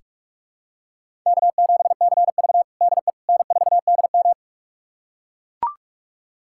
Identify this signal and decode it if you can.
Morse code, heard as a keyed tone, about 42 wpm, 700 Hz. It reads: K7CV DE N4DM
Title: USA Callsign DE Callsign - 4 Characters - 45wpm